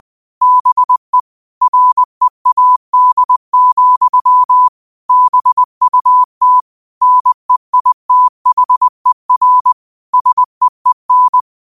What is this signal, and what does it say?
Morse code, heard as a keyed tone, about 20 wpm, 1000 Hz. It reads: BE READ, BUT NEITHER SEEN
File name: qotdfiles / 20wpm